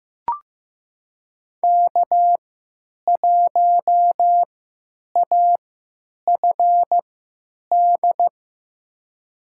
Morse code, heard as a keyed tone, about 15 wpm, spaced as for 9 wpm, 700 Hz. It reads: K1AFD
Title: USA Call Signs - 3x Character Spacing - 15wpm